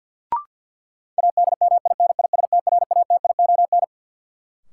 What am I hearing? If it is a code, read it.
Morse code, heard as a keyed tone, about 49 wpm, 700 Hz. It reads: ADMINISTRATION
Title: Sentences from Top 1300 Words - New Word Drill - 50wpm